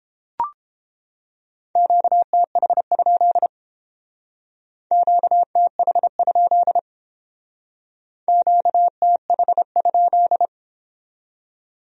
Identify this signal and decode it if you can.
Morse code, heard as a keyed tone, about 30 wpm, 700 Hz. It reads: QTH? QTH? QTH?